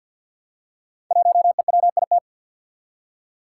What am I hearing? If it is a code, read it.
Morse code, heard as a keyed tone, about 47 wpm, 700 Hz. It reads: 1EWIT
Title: Five Challenging Letter-Number Combinations 50wpm